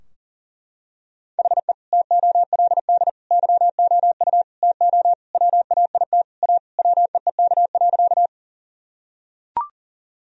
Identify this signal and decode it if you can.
Morse code, heard as a keyed tone, about 40 wpm, 700 Hz. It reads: HE TOLD YOU TO WAIT A WEEK.
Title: Sentences from Top 1000 Words - Review Part 2 - 40wpm